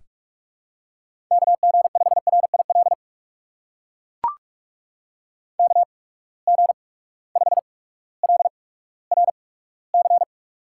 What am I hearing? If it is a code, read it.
Morse code, heard as a keyed tone, about 44 wpm, spaced as for 19 wpm, 700 Hz. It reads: KG5RIL KG5LRC
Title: USA Call Signs - 6 Characters - 8x Character Spacing - 45wpm